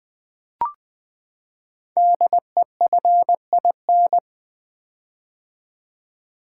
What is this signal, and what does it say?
Morse code, heard as a keyed tone, about 20 wpm, 700 Hz. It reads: DEFIN
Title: Build A Word - 5 Letters At A Time - 10-Letter Words - 20wpm